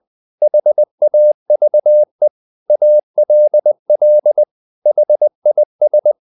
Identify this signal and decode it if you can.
Morse code, heard as a keyed tone, about 20 wpm, 600 Hz. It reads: HAVE ALL HIS
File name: the old man and the sea part 2 20wpm